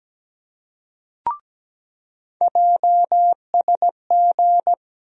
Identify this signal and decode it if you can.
Morse code, heard as a keyed tone, about 17 wpm, 700 Hz. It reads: JSG
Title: USA Call Sign Postfixes - 3 Letters 17wpm